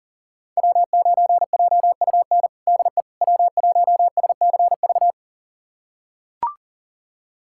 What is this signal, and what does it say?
Morse code, heard as a keyed tone, about 40 wpm, 690 Hz. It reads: W9JUN DE W1SCV
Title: USA Callsign DE Callsign - 5 Characters - 40wpm